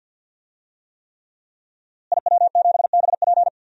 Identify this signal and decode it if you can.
Morse code, heard as a keyed tone, about 52 wpm, 700 Hz. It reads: IW7BP
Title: DXCC-100 Call Signs - 5 Characters - Repeated 3x - 50wpm